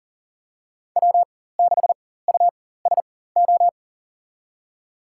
Morse code, heard as a keyed tone, about 40 wpm, spaced as for 19 wpm, 700 Hz. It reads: W6USO
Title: USA Call Signs - 5 Characters - 4x Character Spacing - 40wpm